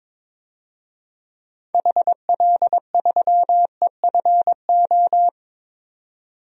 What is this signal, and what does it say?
Morse code, heard as a keyed tone, about 23 wpm, 700 Hz. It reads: HL3EFO
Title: DXCC-100 Call Signs - 6 Characters - 22wpm